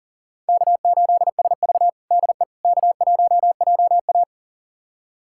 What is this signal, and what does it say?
Morse code, heard as a keyed tone, about 40 wpm, 700 Hz. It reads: K8SV DE K1JA
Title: USA Callsign DE Callsign - 4 Characters - 40wpm